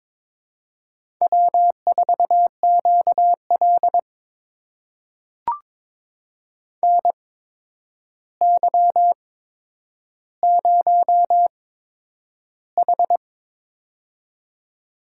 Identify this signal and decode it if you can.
Morse code, heard as a keyed tone, about 22 wpm, spaced as for 9 wpm, 700 Hz. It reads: W4QL NY0H
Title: USA Call Signs - 4 Characters - 8x Character Spacing - 22wpm